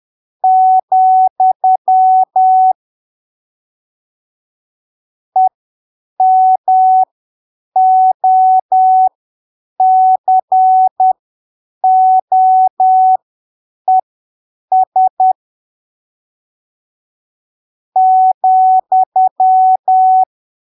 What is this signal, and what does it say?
Morse code, heard as a keyed tone, about 10 wpm, spaced as for 7 wpm, 750 Hz. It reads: , EMOCOES ,
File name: arquivos / texto / texto 18 10ppm